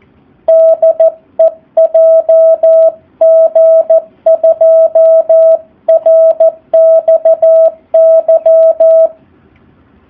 Morse code, heard as a keyed tone, about 14 wpm, 650 Hz. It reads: DE JG2RXY
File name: jh8cbh / 2019cw / SP04- w